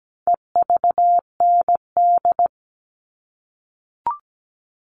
Morse code, heard as a keyed tone, about 17 wpm, 700 Hz. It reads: EVND
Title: All Cut Numbers - 4-Digit 17wpm